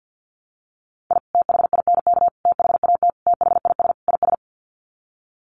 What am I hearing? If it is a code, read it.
Morse code, heard as a keyed tone, about 48 wpm, 700 Hz. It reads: I THINK THAT THIS IS
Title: Sets of 5 Words Encore2 50wpm